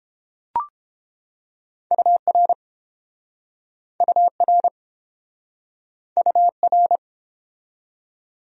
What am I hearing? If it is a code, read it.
Morse code, heard as a keyed tone, about 30 wpm, 700 Hz. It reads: UR UR UR